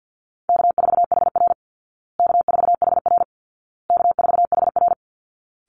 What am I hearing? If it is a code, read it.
Morse code, heard as a keyed tone, about 50 wpm, 700 Hz. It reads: K4HR K4HR K4HR
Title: Call Signs Repeated 2x - 50wpm